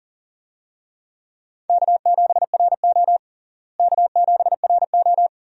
Morse code, heard as a keyed tone, about 37 wpm, 700 Hz. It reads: K7RO K7RO
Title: USA Call Signs - 4 Characters - Repeated 2x - 40wpm